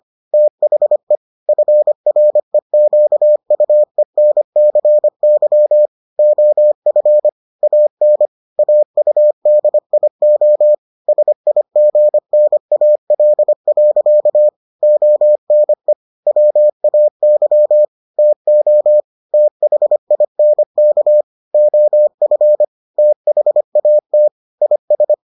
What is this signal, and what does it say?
Morse code, heard as a keyed tone, about 25 wpm, 600 Hz. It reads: THE FREQUENCY OF AN AUDIO SIGNAL. ONE WAY TO THINK OF THAT IS